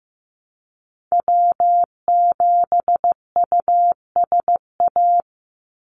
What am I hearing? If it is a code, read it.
Morse code, heard as a keyed tone, about 15 wpm, 700 Hz. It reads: W7USA
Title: NA SOTA Activator Callsigns 15wpm